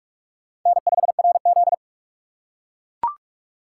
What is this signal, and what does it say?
Morse code, heard as a keyed tone, about 44 wpm, 700 Hz. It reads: N5RZ